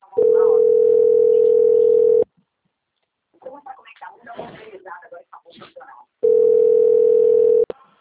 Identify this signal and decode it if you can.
Ringback tone (North America), 440 + 480 Hz, with speech.
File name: out- - - - - .1148